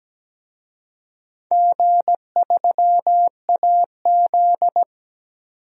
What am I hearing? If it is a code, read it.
Morse code, heard as a keyed tone, about 17 wpm, 700 Hz. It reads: G3AZ
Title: DXCC-100 Call Signs - 4 Characters - 17wpm